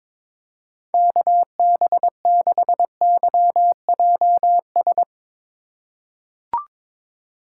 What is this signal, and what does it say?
Morse code, heard as a keyed tone, about 22 wpm, 700 Hz. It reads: KB6YJS
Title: USA Call Signs - 6 Characters - 2x Character Spacing - 22wpm